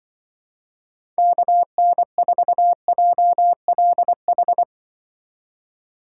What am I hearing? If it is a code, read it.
Morse code, heard as a keyed tone, about 24 wpm, 700 Hz. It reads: KN4JLH
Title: Speed Racing - USA 2x3 Call Signs 20wpm